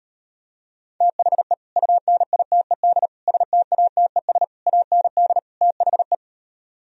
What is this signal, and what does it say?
Morse code, heard as a keyed tone, about 38 wpm, 700 Hz. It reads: THE UNITED STATES AND THE